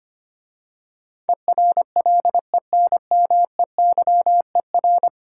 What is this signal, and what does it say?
Morse code, heard as a keyed tone, about 25 wpm, 690 Hz. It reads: ERLENMEYER
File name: practice / 25wpm / Difficult Words - Speed-Racing with Hints - 25wpm